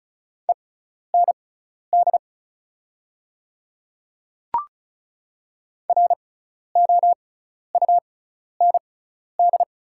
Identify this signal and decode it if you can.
Morse code, heard as a keyed tone, about 36 wpm, spaced as for 12 wpm, 690 Hz. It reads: END ROUND
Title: 6x Character Spacing - Top 200 Words - 35wpm